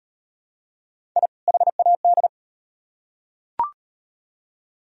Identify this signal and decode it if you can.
Morse code, heard as a keyed tone, about 38 wpm, 700 Hz. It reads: I HAD